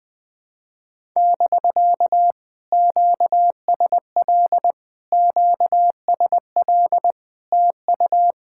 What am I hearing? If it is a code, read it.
Morse code, heard as a keyed tone, about 21 wpm, 700 Hz. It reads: <BK> QSL QSL TU
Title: POTA QSO Practice - 20wpm